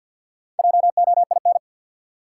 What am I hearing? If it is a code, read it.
Morse code, heard as a keyed tone, about 53 wpm, 700 Hz. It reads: JOIN
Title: Sentences from Top 1400 Words - New Word Drill Alt - 50wpm